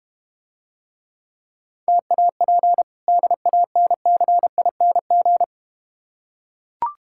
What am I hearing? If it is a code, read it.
Morse code, heard as a keyed tone, about 30 wpm, 690 Hz. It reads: TAP DANCING
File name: practice / 25wpm / Difficult Words - Speed-Racing with Hints - 25wpm